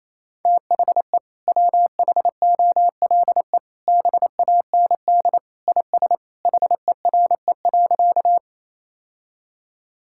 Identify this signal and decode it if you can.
Morse code, heard as a keyed tone, about 28 wpm, 700 Hz. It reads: THE WHOLE BAND IS HERE.
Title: Sentences from Top 1600 Words 28wpm